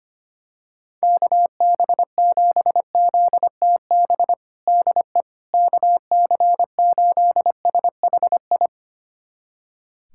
Morse code, heard as a keyed tone, about 25 wpm, 700 Hz. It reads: KB7ZTB DE KC8SHI
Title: USA Callsign DE Callsign - 6 Characters - 25wpm